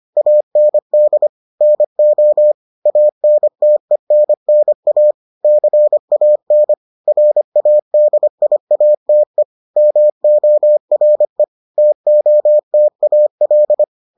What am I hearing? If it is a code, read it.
Morse code, heard as a keyed tone, about 25 wpm, 600 Hz. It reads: AND NO ANTENNA CAN RADIATE MORE TOTAL